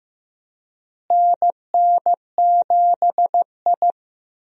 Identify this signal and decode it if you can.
Morse code, heard as a keyed tone, about 15 wpm, 700 Hz. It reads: NN7I
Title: USA Call Signs - 4 Characters - Repeated 2x - 15wpm